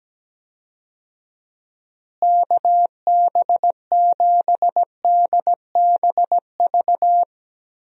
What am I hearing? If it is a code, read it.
Morse code, heard as a keyed tone, about 17 wpm, 700 Hz. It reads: KB7DBV
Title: USA Call Signs - 6 Characters - Repeated 2x - 17wpm